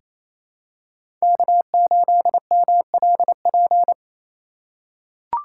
Morse code, heard as a keyed tone, about 28 wpm, 700 Hz. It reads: K8MLP
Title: USA Call Signs - 5 Characters - 2x Character Spacing - 28wpm